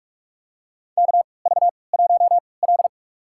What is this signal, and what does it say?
Morse code, heard as a keyed tone, about 44 wpm, spaced as for 25 wpm, 700 Hz. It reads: KV1L
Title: USA Call Signs - 4 Characters - 3x Character Spacing - 45wpm